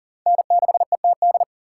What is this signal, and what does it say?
Morse code, heard as a keyed tone, about 50 wpm, 700 Hz. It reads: N6ETD